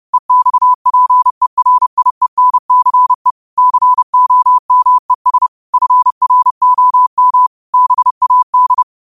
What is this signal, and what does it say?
Morse code, heard as a keyed tone, about 30 wpm, 1000 Hz. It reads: EXPERIENCE COMES FROM BAD
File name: qotdfiles / 30wpm